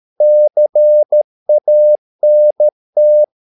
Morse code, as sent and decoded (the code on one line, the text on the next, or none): -.-. .- -. -
CANT